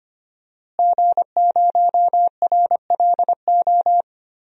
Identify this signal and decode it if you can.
Morse code, heard as a keyed tone, about 25 wpm, 700 Hz. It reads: G0RLO
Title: DXCC-100 Call Signs - 5 Characters - Repeated 3x - 25wpm